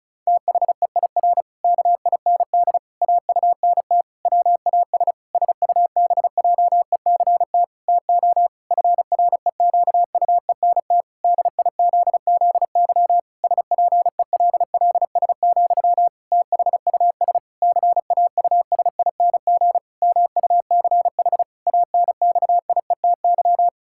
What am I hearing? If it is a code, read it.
Morse code, heard as a keyed tone, about 36 wpm, 700 Hz. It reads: THEIR KIND AUNT WAS SUBJECT TO FREQUENT DIZZY SPELLS, THUS CAUSING MUCH ANXIETY
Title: Pangram Challenge v2 - 35wpm